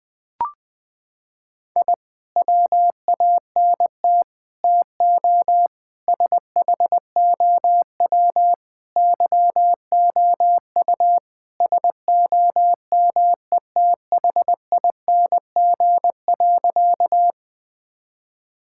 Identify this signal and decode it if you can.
Morse code, heard as a keyed tone, about 20 wpm, 700 Hz. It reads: I WANT TO SHOW YOU SOMETHING.
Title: Sentences with 5-Word Phrases - 20wpm